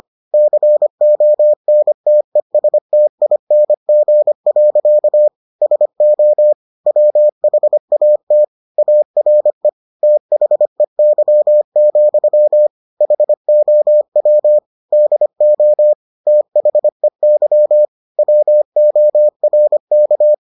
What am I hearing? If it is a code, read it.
Morse code, heard as a keyed tone, about 25 wpm, 600 Hz. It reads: CONTESTING. SO WHAT ARE THEY, HOW DO THEY WORK